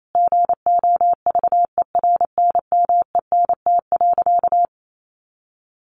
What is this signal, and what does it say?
Morse code, heard as a keyed tone, about 28 wpm, 700 Hz. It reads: GOVERNMENT.
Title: Sentences from Top 1400 Words 28wpm